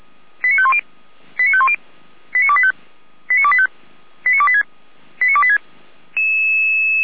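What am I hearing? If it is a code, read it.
Five-tone selective call (ZVEI), heard as quick runs of five tones, tones of about 80 ms. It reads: 89510 89510 89285 89285 89286 89286
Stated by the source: ZVEI five-tone selective call 89223 received by SDR, from the Feuerwehr Orlamünde audio server.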